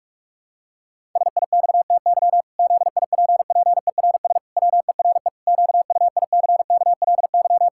Morse code, heard as a keyed tone, about 46 wpm, 700 Hz. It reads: SIXTY ZIPPERS WERE QUICKLY
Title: Pangram Challenge v2 - 45wpm